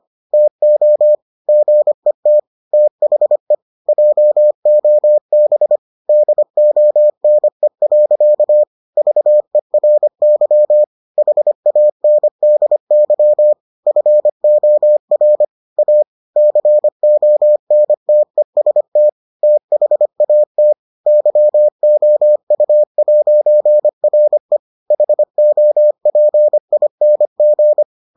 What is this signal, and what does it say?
Morse code, heard as a keyed tone, about 25 wpm, 600 Hz. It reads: TO GET THE JOB DONE. VERY HANDY FOR A CONTEST THAT YOU'RE HOPING